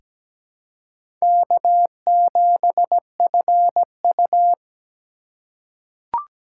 Morse code, heard as a keyed tone, about 17 wpm, 700 Hz. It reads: K7FU
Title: USA Call Signs - 4 Characters - 4x Character Spacing - 17wpm